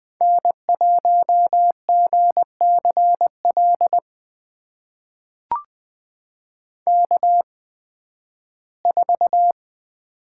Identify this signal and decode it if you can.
Morse code, heard as a keyed tone, about 20 wpm, spaced as for 13 wpm, 700 Hz. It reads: N1GCL K4
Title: USA Call Signs - 5 Characters - 8x Character Spacing - 20wpm